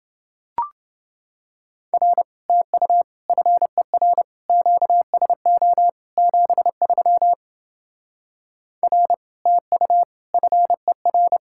Morse code, heard as a keyed tone, about 29 wpm, 700 Hz. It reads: R TU FER QSO 73 R TU FER